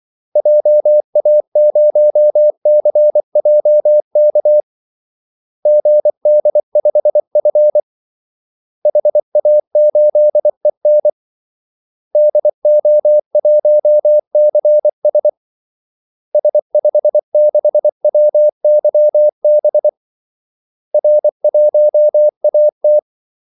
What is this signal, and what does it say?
Morse code, heard as a keyed tone, about 24 wpm, 600 Hz. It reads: JA0CJK GD5F HA8EN DO1CS S56WYB R1AT
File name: Callsigns-24wpm